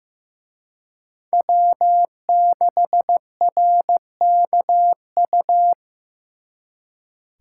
Morse code, heard as a keyed tone, about 18 wpm, 700 Hz. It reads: W6RKU
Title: Speed Racing - USA 1x3 Call Signs 15wpm